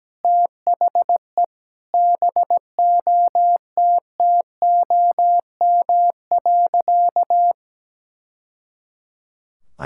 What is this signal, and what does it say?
Morse code, heard as a keyed tone, about 17 wpm, 700 Hz. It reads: THE BOTTOM.
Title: Sentences from Top 1000 Words - Review Part 6 - 17wpm